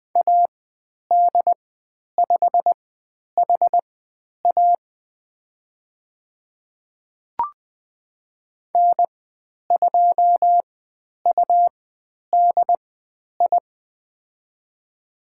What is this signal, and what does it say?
Morse code, heard as a keyed tone, about 20 wpm, spaced as for 10 wpm, 700 Hz. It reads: AD5HA N2UDI
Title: 102 Lesson 17g - Letter u - Call Signs 20wpm